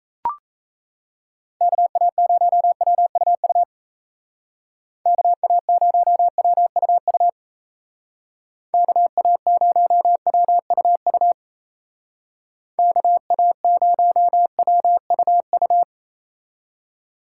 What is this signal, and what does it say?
Morse code, heard as a keyed tone, about 34 wpm, 700 Hz. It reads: KA0WUU KA0WUU KA0WUU KA0WUU
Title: Speed Racing - USA Call Signs 28wpm